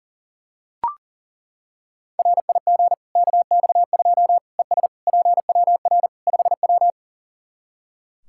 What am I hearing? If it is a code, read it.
Morse code, heard as a keyed tone, about 40 wpm, 700 Hz. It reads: RIG KX2 ES PWR 5W
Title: Practice QSO - Exchange 2 - Info - 40wpm